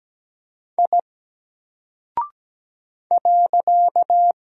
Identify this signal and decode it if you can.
Morse code, heard as a keyed tone, about 17 wpm, 700 Hz. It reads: I .
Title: Single Character (Letter:Number:Punctuation:Prosign) Alt 17wpm